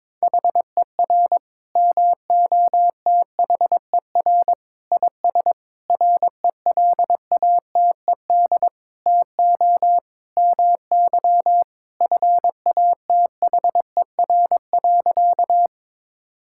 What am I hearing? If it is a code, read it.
Morse code, heard as a keyed tone, about 22 wpm, 700 Hz. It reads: HER MOTHER IS RELATED TO MY FATHER.
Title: Sentences from Top 1000 Words - Review Part 5 - 22wpm